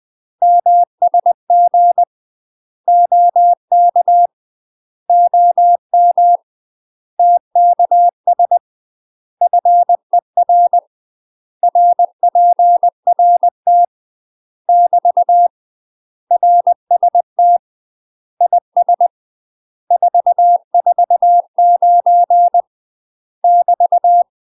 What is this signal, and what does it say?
Morse code, heard as a keyed tone, about 20 wpm, 700 Hz. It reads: MSG OK OM TKS FER RPRT <BT> RST IS 449 <BT>
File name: UK MASTER QSOs TRAINING / 20WPM / UK MASTER QSOs 0004